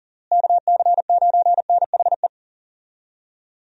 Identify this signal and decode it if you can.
Morse code, heard as a keyed tone, about 40 wpm, 700 Hz. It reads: KC9NHE
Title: USA Call Signs - 6 Characters - Repeated 3x - 40wpm